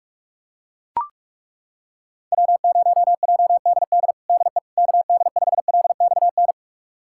Morse code, heard as a keyed tone, about 43 wpm, 700 Hz. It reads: W0JDD DE KD5LXN
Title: USA Callsign DE Callsign 45wpm